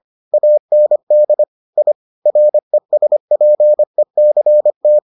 Morse code, heard as a keyed tone, about 25 wpm, 600 Hz. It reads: AND I RESPECT